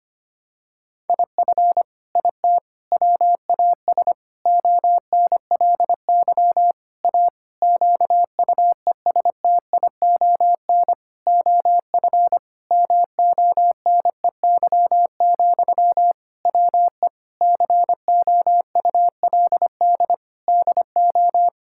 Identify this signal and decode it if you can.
Morse code, heard as a keyed tone, about 25 wpm, 700 Hz. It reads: IF IT WAS ONLY A QUESTION OF MONEY, WE COULD DO
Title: Sentences from Top 1600 Words 25wpm